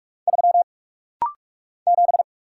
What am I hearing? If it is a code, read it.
Morse code, heard as a keyed tone, about 44 wpm, 700 Hz. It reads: 3 7
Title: Single Letter-Number - Rapid-Fire 45wpm